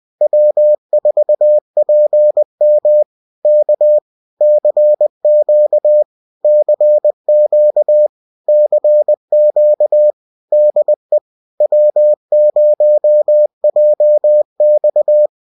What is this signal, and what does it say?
Morse code, heard as a keyed tone, about 20 wpm, 600 Hz. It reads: W4PM K CQ CQ CQ DE W0JX